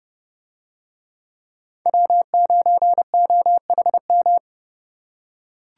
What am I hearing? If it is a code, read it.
Morse code, heard as a keyed tone, about 29 wpm, 700 Hz. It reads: W9OHM